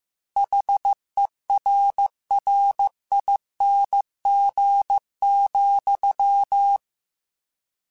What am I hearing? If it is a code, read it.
Morse code, heard as a keyed tone, about 15 wpm, 800 Hz. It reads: HERRING,
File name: Misc / Misc-mp3s / Word Pairs @13